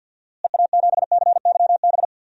Morse code, heard as a keyed tone, about 49 wpm, 700 Hz. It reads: EA7CYB
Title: DXCC-100 Call Signs - 6 Characters - Repeated 3x - 50wpm